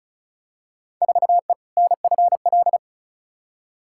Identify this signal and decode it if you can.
Morse code, heard as a keyed tone, about 36 wpm, 690 Hz. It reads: 4E NFL
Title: Field Day Exchanges Encore - 35wpm